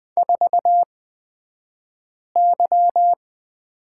Morse code, heard as a keyed tone, about 20 wpm, spaced as for 5 wpm, 700 Hz. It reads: 4Y
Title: 249 Lesson 40d - BK - Two Characters 20wpm